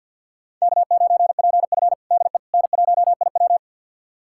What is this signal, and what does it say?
Morse code, heard as a keyed tone, about 52 wpm, 700 Hz. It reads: K9PF DE N1IW
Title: USA Callsign DE Callsign - 4 Characters - 50wpm